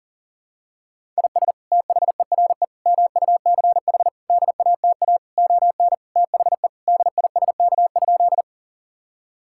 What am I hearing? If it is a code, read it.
Morse code, heard as a keyed tone, about 40 wpm, 700 Hz. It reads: IS THERE MUCH DATA ON THE DISK?
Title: Sentences from Top 1000 Words - Review Part 6 - 40wpm